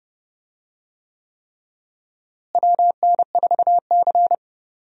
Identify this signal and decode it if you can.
Morse code, heard as a keyed tone, about 30 wpm, 700 Hz. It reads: WN4C
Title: USA Call Signs - 4 Characters - Repeated 3x - 30wpm